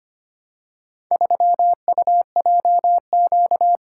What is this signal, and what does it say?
Morse code, heard as a keyed tone, about 25 wpm, 700 Hz. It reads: 3UJQ
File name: practice / 25wpm / Four Letter-Number Combinations - 25wpm